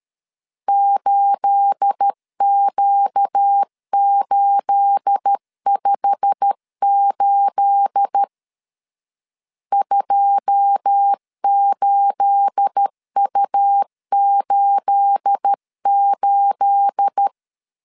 Morse code, heard as a keyed tone, about 13 wpm, 800 Hz. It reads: 8Q858 28U88
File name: LEZIONI / 9 28 12wpm 2351